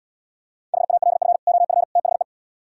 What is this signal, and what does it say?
Morse code, heard as a keyed tone, about 77 wpm, 700 Hz. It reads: HALF OF THE